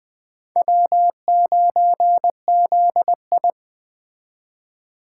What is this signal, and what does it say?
Morse code, heard as a keyed tone, about 18 wpm, 700 Hz. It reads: W9ZI
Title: Speed Racing - USA 1x2 Call Signs 15wpm